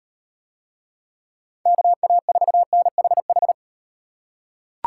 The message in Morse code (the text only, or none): KA4NHH